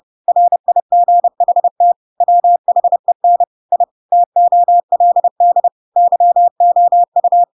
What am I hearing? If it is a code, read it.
Morse code, heard as a keyed tone, about 30 wpm, 700 Hz. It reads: RIGHT WHEN I TOLD YOU
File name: Pinocchio / Pinocchio ch7 30 30 ik1hge